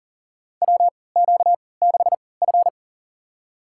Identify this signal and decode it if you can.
Morse code, heard as a keyed tone, about 40 wpm, spaced as for 23 wpm, 700 Hz. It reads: WQ6F
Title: USA Call Signs - 4 Characters - 3x Character Spacing - 40wpm